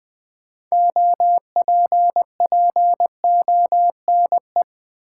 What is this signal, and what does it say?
Morse code, heard as a keyed tone, about 20 wpm, 700 Hz. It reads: OPPONE
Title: Build A Word - 2 Letters At A Time - 8-Letter Words - 20wpm